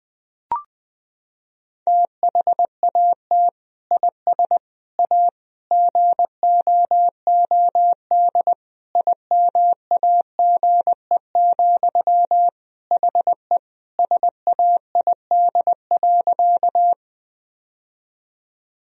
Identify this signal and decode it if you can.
Morse code, heard as a keyed tone, about 20 wpm, 700 Hz. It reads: THAT IS A GOOD IMAGE, HE SAID.
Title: Sentences from Top 1000 Words - Review Part 4 - 20wpm